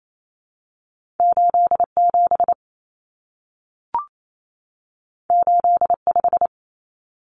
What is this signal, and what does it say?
Morse code, heard as a keyed tone, about 28 wpm, 700 Hz. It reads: 87 85
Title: Common Cut Numbers - 2-Digit 28wpm